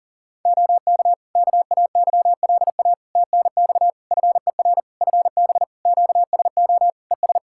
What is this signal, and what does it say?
Morse code, heard as a keyed tone, about 40 wpm, 700 Hz. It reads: OK KAYLA TNX FER FB QSO ES
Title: K1USN SST QSO Practice 40wpm